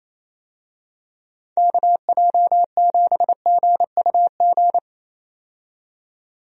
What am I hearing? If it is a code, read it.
Morse code, heard as a keyed tone, about 28 wpm, 700 Hz. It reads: KJ7GUG
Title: USA Call Signs - 6 Characters - 28wpm